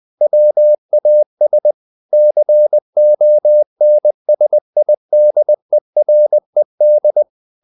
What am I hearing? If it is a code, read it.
Morse code, heard as a keyed tone, about 20 wpm, 600 Hz. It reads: WAS CONSIDERED
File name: the old man and the sea / the old man and the sea part 1 20wpm